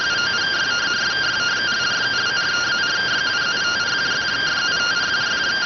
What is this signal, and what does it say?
Two-tone teleprinter signal, about 170 Hz shift.